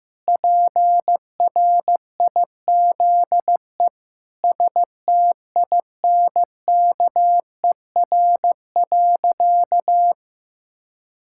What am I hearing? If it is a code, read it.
Morse code, heard as a keyed tone, about 16 wpm, 700 Hz. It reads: PRIZE STINKER.
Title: Pangram Challenge v2 - 15wpm